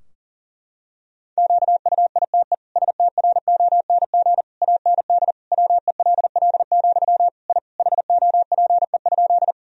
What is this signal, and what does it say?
Morse code, heard as a keyed tone, about 40 wpm, 700 Hz. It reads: QUITE STRONG AND WELL, I HOPE?
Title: Sentences from Top 1000 Words - Review Part 3 - 40wpm